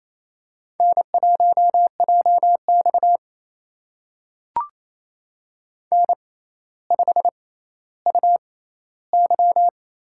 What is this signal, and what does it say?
Morse code, heard as a keyed tone, about 27 wpm, spaced as for 14 wpm, 700 Hz. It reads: N1JX N5UY
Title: USA Call Signs - 4 Characters - 6x Character Spacing - 28wpm